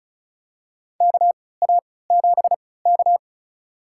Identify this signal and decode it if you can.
Morse code, heard as a keyed tone, about 36 wpm, spaced as for 20 wpm, 700 Hz. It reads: KA7K
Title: USA Call Signs - 3x Character Spacing - 35wpm